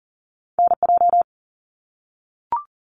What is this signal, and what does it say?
Morse code, heard as a keyed tone, about 42 wpm, 700 Hz. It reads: NJ
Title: US States 40wpm